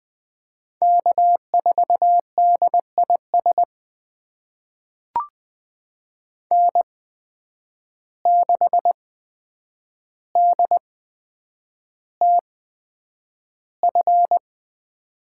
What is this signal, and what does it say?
Morse code, heard as a keyed tone, about 20 wpm, spaced as for 9 wpm, 700 Hz. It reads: K4DIS N6DTF
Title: USA Call Signs - 5 Characters - 8x Character Spacing - 20wpm